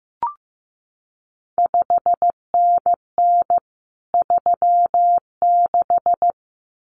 Morse code, heard as a keyed tone, about 15 wpm, 700 Hz. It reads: 5NN 36
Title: CQ WW DX Contest Exchange Encore - 15wpm